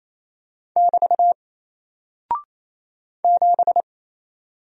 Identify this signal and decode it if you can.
Morse code, heard as a keyed tone, about 27 wpm, 700 Hz. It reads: <BT> 7
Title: Single Character (Letter:Number:Punctuation:Prosign) 28wpm